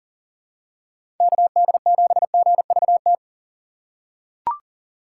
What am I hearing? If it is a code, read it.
Morse code, heard as a keyed tone, about 40 wpm, 700 Hz. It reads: KD7GVT
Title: USA Call Signs - 6 Characters - 6x Character Spacing - 40wpm